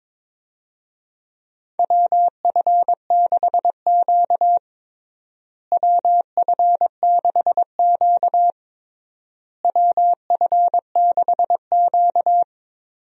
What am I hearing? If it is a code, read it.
Morse code, heard as a keyed tone, about 21 wpm, 700 Hz. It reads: WF6Q WF6Q WF6Q
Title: USA Call Signs - 4 Characters - Repeated 3x - 22wpm